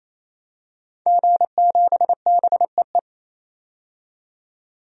Morse code, heard as a keyed tone, about 28 wpm, 700 Hz. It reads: G7BEE
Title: DXCC-100 Call Signs - 5 Characters - 28wpm